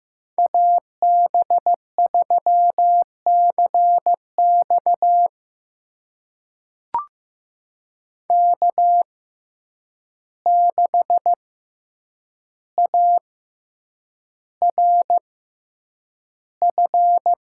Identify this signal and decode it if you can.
Morse code, heard as a keyed tone, about 15 wpm, spaced as for 8 wpm, 700 Hz. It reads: AB3CX K6ARF
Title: USA Call Signs - 5 Characters - 6x Character Spacing - 15wpm